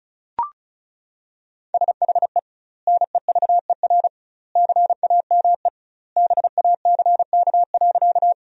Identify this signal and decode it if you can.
Morse code, heard as a keyed tone, about 36 wpm, 700 Hz. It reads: SHE NEVER CAME BACK.
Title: 2x Word-Spacing - Sentences from Top 300 Words - 35wpm